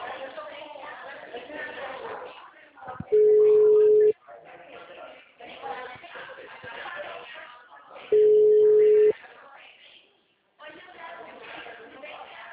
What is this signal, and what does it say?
Ringback tone (Europe, 425 Hz), with speech.
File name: out- -2014- - - .323